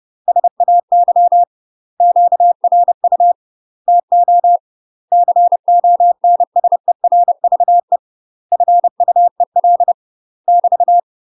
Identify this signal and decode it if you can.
Morse code, heard as a keyed tone, about 30 wpm, 700 Hz. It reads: SAY QRU TO CONSERVE FUEL <BT>